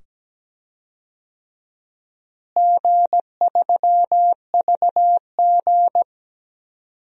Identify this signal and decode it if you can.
Morse code, heard as a keyed tone, about 17 wpm, 700 Hz. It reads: G3VG
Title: DXCC-100 Call Signs - 4 Characters - Repeated 2x - 17wpm